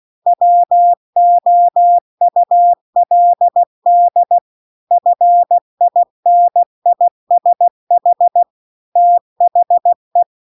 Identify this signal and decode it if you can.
Morse code, heard as a keyed tone, about 16 wpm, 700 Hz. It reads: WOULD FINISH THE
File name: CW-The-Big-Twoheaded-River-2-16WPM